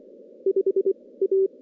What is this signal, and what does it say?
Morse code, heard as a keyed tone, about 25 wpm, 380 Hz. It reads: 5 A
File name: contests / CW-SS-k5tr- -2300